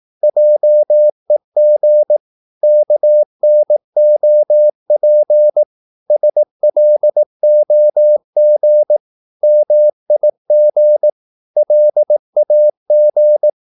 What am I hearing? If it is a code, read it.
Morse code, heard as a keyed tone, about 18 wpm, 600 Hz. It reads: JEG KNOP SLOG MIG LAG
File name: Korte ord DK 18wpm